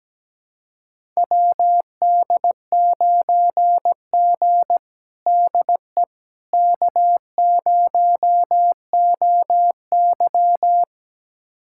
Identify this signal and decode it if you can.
Morse code, heard as a keyed tone, about 17 wpm, 700 Hz. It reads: WD9G DE K0OY
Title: USA Callsign DE Callsign - 4 Characters - 17wpm